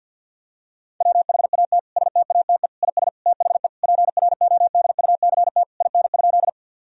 Morse code, heard as a keyed tone, about 50 wpm, 700 Hz. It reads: WHAT STATE IS THE PRODUCT IN?
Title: Sentences from Top 1000 Words - Review Part 2 - 50wpm